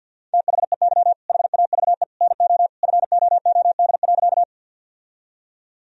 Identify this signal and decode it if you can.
Morse code, heard as a keyed tone, about 52 wpm, 700 Hz. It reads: THEY HAVE NO FOOD.
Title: Sentences from Top 1000 Words - Review Part 1 - 50wpm